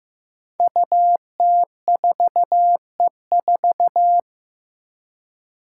Morse code, heard as a keyed tone, about 15 wpm, 700 Hz. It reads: UT4E4
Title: Five Challenging Letter-Number Combinations 15wpm